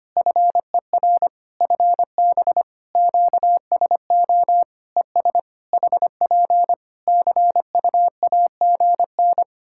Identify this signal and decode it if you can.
Morse code, heard as a keyed tone, about 25 wpm, 700 Hz. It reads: FER FB QSO ES HP CUAGN
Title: Practice QSO - Exchange 4 - Ending - 25wpm